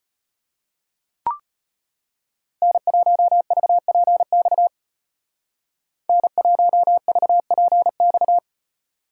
Morse code, heard as a keyed tone, about 33 wpm, 700 Hz. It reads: N1VPX N1VPX